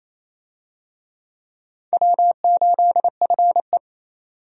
Morse code, heard as a keyed tone, about 27 wpm, 700 Hz. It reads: W8FE
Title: USA Call Signs - 4 Characters - 28wpm